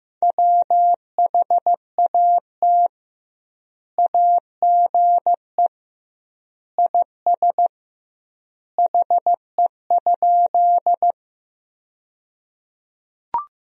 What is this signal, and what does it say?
Morse code, heard as a keyed tone, about 16 wpm, 700 Hz. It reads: WHAT AGE IS HE?
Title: 2x Word-Spacing - Sentences from Top 500 Words - 15wpm